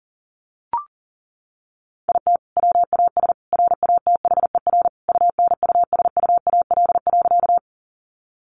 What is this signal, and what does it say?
Morse code, heard as a keyed tone, about 40 wpm, 700 Hz. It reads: IT WAS RATHER UNUSUAL.
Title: Sentences from Top 1400 Words 40wpm